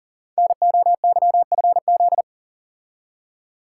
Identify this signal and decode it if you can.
Morse code, heard as a keyed tone, about 38 wpm, 700 Hz. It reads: NOYFZ
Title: Code Groups - 5 Letters - 40wpm